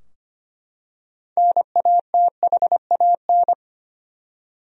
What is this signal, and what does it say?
Morse code, heard as a keyed tone, about 25 wpm, 700 Hz. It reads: NATHAN
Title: USA Common Names - 25wpm